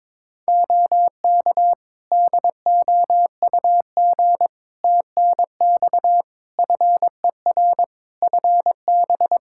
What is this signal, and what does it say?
Morse code, heard as a keyed tone, about 22 wpm, 700 Hz. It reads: OK DOUG TNX FER FB
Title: Practice QSO - Exchange 4 - Ending - 22wpm